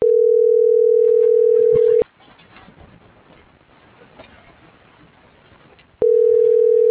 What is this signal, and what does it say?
Ringback tone (North America), 440 + 480 Hz.